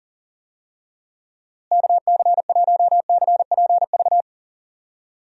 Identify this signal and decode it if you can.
Morse code, heard as a keyed tone, about 40 wpm, 700 Hz. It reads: KC1CPV
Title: USA Call Signs - 6 Characters - Repeated 3x - 40wpm